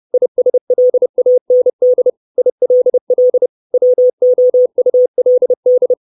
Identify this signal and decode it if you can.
Morse code, heard as a keyed tone, about 30 wpm, 500 Hz. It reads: ISLAND ILL WOULD
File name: 632 words / 632 Words 30WPM